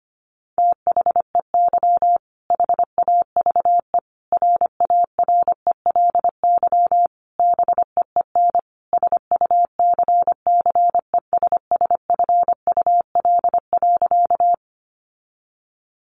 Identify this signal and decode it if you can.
Morse code, heard as a keyed tone, about 25 wpm, 700 Hz. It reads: THEY HAVE RARELY BEEN SUCCESSFUL.
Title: Sentences from Top 1400 Words 25wpm